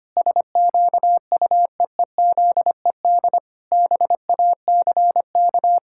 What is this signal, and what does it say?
Morse code, heard as a keyed tone, about 25 wpm, 700 Hz. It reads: SQUEEZED BACK
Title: Pangram Challenge v2 - 25wpm